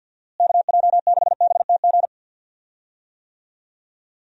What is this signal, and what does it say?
Morse code, heard as a keyed tone, about 51 wpm, 700 Hz. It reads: KJ6BTG
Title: USA Call Signs - 6 Characters - Repeated 3x - 50wpm